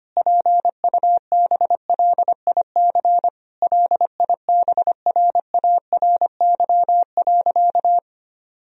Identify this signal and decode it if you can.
Morse code, heard as a keyed tone, about 25 wpm, 700 Hz. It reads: PUBLIC LIBRARY.
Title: Sentences from Top 1000 Words - Review Part 4 - 25wpm